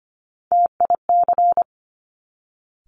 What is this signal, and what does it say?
Morse code, heard as a keyed tone, about 25 wpm, 700 Hz. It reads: TIC
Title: Top Three Letter Combinations in Words 25wpm